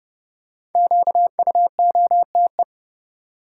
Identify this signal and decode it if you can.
Morse code, heard as a keyed tone, about 30 wpm, 700 Hz. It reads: QUOTE
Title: Five Letter Words - 30wpm